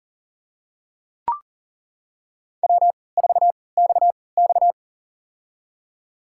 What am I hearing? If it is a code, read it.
Morse code, heard as a keyed tone, about 41 wpm, spaced as for 23 wpm, 700 Hz. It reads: W4XX
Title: USA Call Signs - 4 Characters - 3x Character Spacing - 40wpm